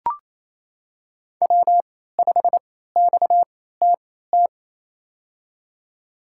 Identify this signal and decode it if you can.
Morse code, heard as a keyed tone, about 28 wpm, spaced as for 16 wpm, 700 Hz. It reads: W5XTT
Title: USA Call Signs - 3x Character Spacing - 28wpm